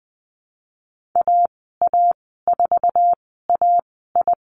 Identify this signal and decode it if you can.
Morse code, heard as a keyed tone, about 20 wpm, spaced as for 15 wpm, 700 Hz. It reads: AA4AI
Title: 048 Lesson 9g - Number 4 - Call Signs 20wpm